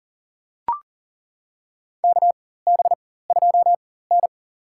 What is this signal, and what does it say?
Morse code, heard as a keyed tone, about 40 wpm, spaced as for 19 wpm, 700 Hz. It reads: KB2N
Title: USA Call Signs - 4 Characters - 4x Character Spacing - 40wpm